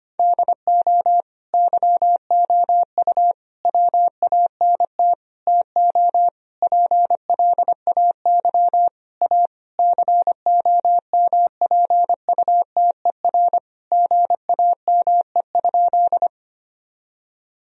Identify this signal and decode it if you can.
Morse code, heard as a keyed tone, about 25 wpm, 700 Hz. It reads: DO YOU WANT TO PLAY A COMPUTER GAME?
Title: Sentences from Top 1000 Words - Review Part 3 - 25wpm